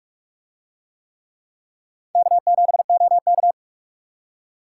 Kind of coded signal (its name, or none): Morse code